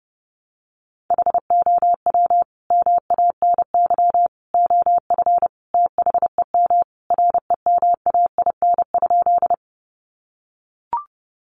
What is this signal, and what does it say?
Morse code, heard as a keyed tone, about 30 wpm, 700 Hz. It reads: HOW MANY OF THEM REMAIN?
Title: Sentences from Top 1200 Words 30wpm